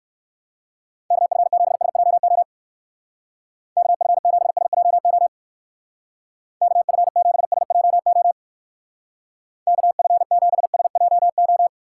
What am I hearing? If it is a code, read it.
Morse code, heard as a keyed tone, about 57 wpm, 700 Hz. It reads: KF7SJO KF7SJO KF7SJO KF7SJO